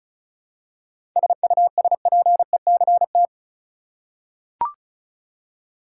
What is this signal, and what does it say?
Morse code, heard as a keyed tone, about 34 wpm, 700 Hz. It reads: SUSPECT
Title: Seven Letter Words - 35wpm